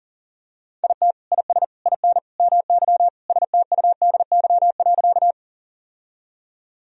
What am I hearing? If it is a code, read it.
Morse code, heard as a keyed tone, about 40 wpm, 700 Hz. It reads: IT IS IN MY STUDY.
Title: Sentences from Top 1000 Words - Review Part 1 - 40wpm